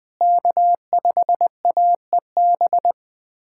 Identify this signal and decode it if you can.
Morse code, heard as a keyed tone, about 20 wpm, 700 Hz. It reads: K5AEB
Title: K1USN SST QSO Practice - 20wpm